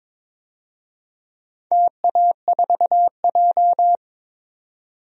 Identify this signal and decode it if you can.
Morse code, heard as a keyed tone, about 22 wpm, 700 Hz. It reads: TA4J
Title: DXCC-100 Call Signs - 4 Characters - 22wpm